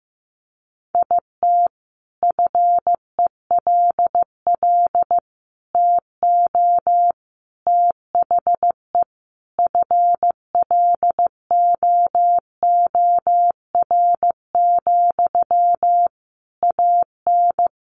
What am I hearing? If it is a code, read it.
Morse code, heard as a keyed tone, about 15 wpm, 700 Hz. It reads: IT FELL TO THE FLOOR, AN
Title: A Sound Of Thunder 15wpm